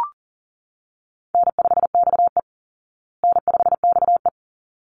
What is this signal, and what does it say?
Morse code, heard as a keyed tone, about 40 wpm, 700 Hz. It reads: N5XE N5XE
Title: Easy Call Signs Repeated 2x - 40wpm